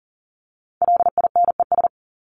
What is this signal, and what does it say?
Morse code, heard as a keyed tone, about 40 wpm, 700 Hz. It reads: LINES
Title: ES:S Suffix - Top 100 Words with Base Words 40wpm